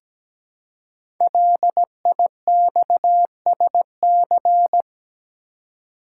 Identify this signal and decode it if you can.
Morse code, heard as a keyed tone, about 17 wpm, 700 Hz. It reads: LIXSC
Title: Code Groups - 5 Letters - 17wpm